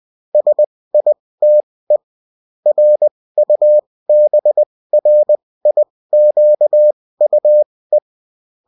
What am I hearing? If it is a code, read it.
Morse code, heard as a keyed tone, about 20 wpm, spaced as for 16 wpm, 600 Hz. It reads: SITE RUBRIQUE